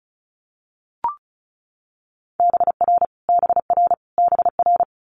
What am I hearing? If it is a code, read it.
Morse code, heard as a keyed tone, about 36 wpm, 700 Hz. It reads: BR BR BR
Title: Multi-letter Phonemes Repeated 3x 35wpm